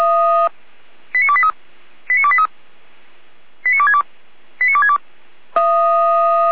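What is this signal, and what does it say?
Five-tone selective call (ZVEI), heard as quick runs of five tones, tones of about 70 ms. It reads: 89283 89283 89262 89262
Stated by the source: ZVEI five-tone selective call received by SDR, from the feuerwehr-orlamuende.org fire brigade server, address 89001.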